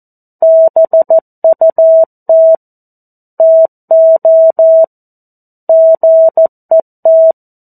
Morse code, heard as a keyed tone, about 14 wpm, 650 Hz. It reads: BUT TO GET
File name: 02-Chapter-02 14 WPM